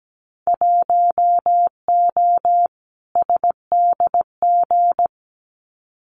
Morse code, heard as a keyed tone, about 17 wpm, 700 Hz. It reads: 1O SDG
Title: Winter Field Day Exchanges Encore 17wpm